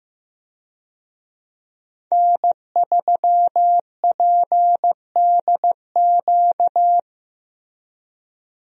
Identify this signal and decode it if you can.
Morse code, heard as a keyed tone, about 15 wpm, 700 Hz. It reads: N3PDQ
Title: USA Call Signs - 5 Characters - Repeated 2x - 15wpm